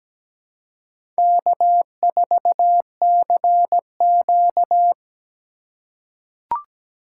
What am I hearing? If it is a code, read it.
Morse code, heard as a keyed tone, about 17 wpm, 700 Hz. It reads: K4CQ
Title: USA Call Signs - 4 Characters - 4x Character Spacing - 17wpm